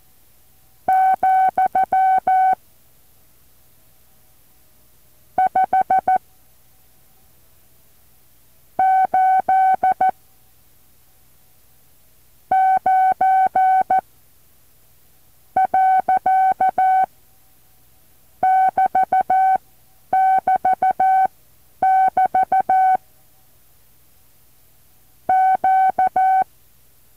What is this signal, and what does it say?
Morse code, heard as a keyed tone, about 14 wpm, spaced as for 7 wpm, 760 Hz. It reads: , 5 8 9 .<BT><BT><BT> Q
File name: intl morse code2